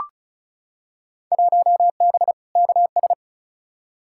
Morse code, heard as a keyed tone, about 36 wpm, 700 Hz. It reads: 1B KS
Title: Field Day Exchanges Encore - 35wpm